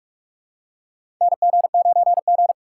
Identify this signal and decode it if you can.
Morse code, heard as a keyed tone, about 46 wpm, 690 Hz. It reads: NG9G
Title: USA Call Signs - 4 Characters - 3x Character Spacing - 45wpm